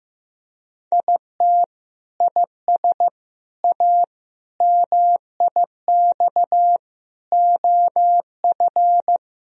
Morse code, heard as a keyed tone, about 15 wpm, 700 Hz. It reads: IT IS A MIX OF
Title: Sentences from Top 1000 Words - Review Part 4 - 15wpm